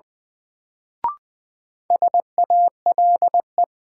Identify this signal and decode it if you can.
Morse code, heard as a keyed tone, about 20 wpm, 700 Hz. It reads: SALE
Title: Sentences from Top 1400 Words - New Word Drill - 20wpm